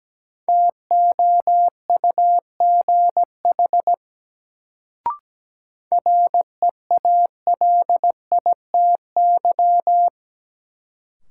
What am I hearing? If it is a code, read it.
Morse code, heard as a keyed tone, about 17 wpm, 700 Hz. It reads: TOUGH REALITY
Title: Sentences from Top 1200 Words - New Word Drill - 17wpm